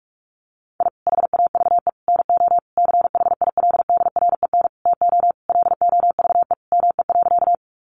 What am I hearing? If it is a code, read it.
Morse code, heard as a keyed tone, about 44 wpm, 700 Hz. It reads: I HAVE NO CHILDREN TO LOVE ME.
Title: Sentences from Top 400 Words 45wpm